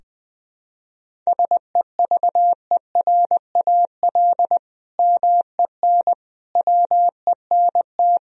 Morse code, heard as a keyed tone, about 21 wpm, 700 Hz. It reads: SEVERAL MEN WENT
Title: Sentences from Top 1000 Words - Review Part 2 - 20wpm